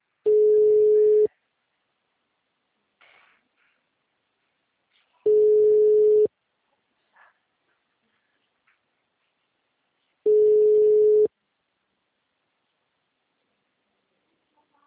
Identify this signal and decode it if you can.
Ringback tone (Europe, 425 Hz).